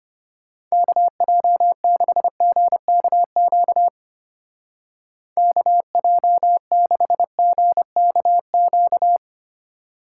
Morse code, heard as a keyed tone, about 32 wpm, 700 Hz. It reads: KJ6GKQ KJ6GKQ